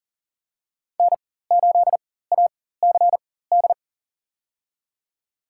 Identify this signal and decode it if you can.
Morse code, heard as a keyed tone, about 40 wpm, spaced as for 19 wpm, 700 Hz. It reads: N8ACD
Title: USA Call Signs - 4x Character Spacing - 40wpm